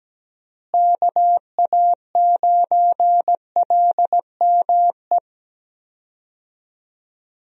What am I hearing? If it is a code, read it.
Morse code, heard as a keyed tone, about 17 wpm, 700 Hz. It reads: KA9LME
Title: USA Call Signs - 6 Characters - Repeated 3x - 17wpm